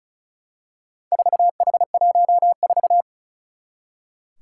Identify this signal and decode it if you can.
Morse code, heard as a keyed tone, about 36 wpm, 700 Hz. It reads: 4H14
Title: Four Challenging Letter-Number Combinations 35wpm